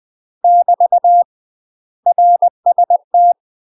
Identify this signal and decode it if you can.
Morse code, heard as a keyed tone, about 20 wpm, 700 Hz. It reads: <BT> RST
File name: UK MASTER QSOs TRAINING / 20WPM / UK MASTER QSOs 0015